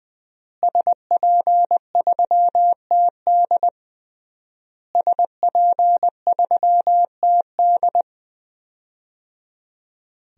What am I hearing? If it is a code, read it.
Morse code, heard as a keyed tone, about 20 wpm, 700 Hz. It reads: SP3TD SP3TD
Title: DXCC-100 Call Signs - 5 Characters - Repeated 3x - 20wpm